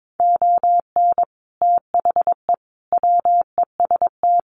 Morse code, heard as a keyed tone, about 22 wpm, 700 Hz. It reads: ON THE WEST
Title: Sets of 5 Words Encore2 22wpm